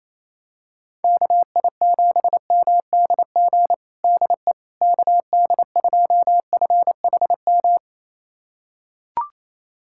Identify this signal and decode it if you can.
Morse code, heard as a keyed tone, about 28 wpm, 700 Hz. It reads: KI7MDG DE KD2FHM
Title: USA Callsign DE Callsign 28wpm